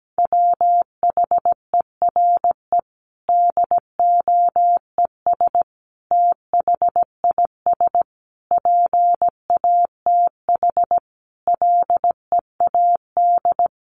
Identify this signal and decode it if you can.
Morse code, heard as a keyed tone, about 17 wpm, 700 Hz. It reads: WHERE DOES THIS PATH LEAD
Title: Sentences from Top 1400 Words 17wpm